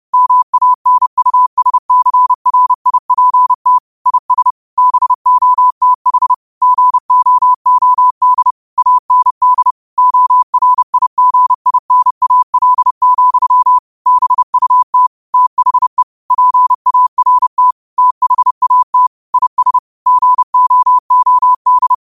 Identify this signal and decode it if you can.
Morse code, heard as a keyed tone, about 30 wpm, 1000 Hz. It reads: MANUSCRIPT IS BOTH GOOD AND ORIGINAL, BUT THE PART THAT IS GOOD
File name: qotdfiles / 30wpm